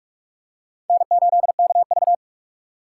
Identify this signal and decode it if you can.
Morse code, heard as a keyed tone, about 42 wpm, 700 Hz. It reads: N8KV
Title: USA Call Signs - 4 Characters - 3x Character Spacing - 45wpm